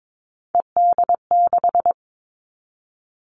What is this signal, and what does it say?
Morse code, heard as a keyed tone, about 22 wpm, 700 Hz. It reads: ED6
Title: Callsign Prefixes for Top 50 DXCC - 22wpm